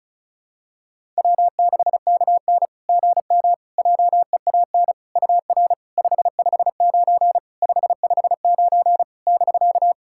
Morse code, heard as a keyed tone, about 36 wpm, 690 Hz. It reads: W6KN GM JEAN UR 559 559 <BK>
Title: SOTA QSO Practice 35wpm